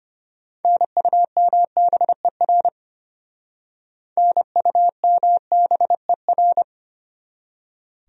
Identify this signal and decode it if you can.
Morse code, heard as a keyed tone, about 32 wpm, 700 Hz. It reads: NUMBER NUMBER